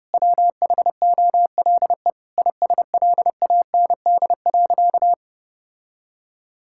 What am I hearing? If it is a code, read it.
Morse code, heard as a keyed tone, about 30 wpm, 700 Hz. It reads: WHOLE ISLAND.
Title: Sentences from Top 1000 Words - Review Part 5 - 30wpm